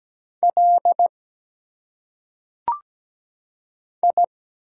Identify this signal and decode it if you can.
Morse code, heard as a keyed tone, about 17 wpm, 700 Hz. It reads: L I
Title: Single Letters - Phonetic Pronunciation 17wpm